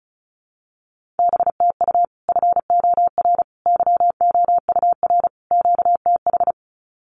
Morse code, heard as a keyed tone, about 34 wpm, 700 Hz. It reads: BTU FOR YOUR QTH
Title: Common QSO Phrases 35wpm